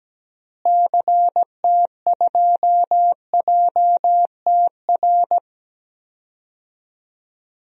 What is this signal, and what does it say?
Morse code, heard as a keyed tone, about 17 wpm, 700 Hz. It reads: CT2JTR
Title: DXCC-100 Call Signs - 6 Characters - Repeated 3x - 17wpm